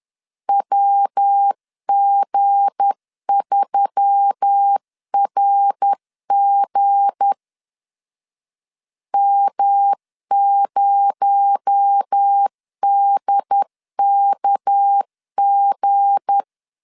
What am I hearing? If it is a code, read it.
Morse code, heard as a keyed tone, about 10 wpm, 800 Hz. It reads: WG3RG M0DKG
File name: LEZIONI / 7 GF9Z / 7 G 10wpm 2501